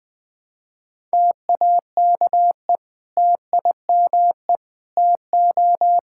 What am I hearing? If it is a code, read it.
Morse code, heard as a keyed tone, about 20 wpm, 700 Hz. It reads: TAKE TIME TO
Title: Sentences with 3-Word Phrases - 20wpm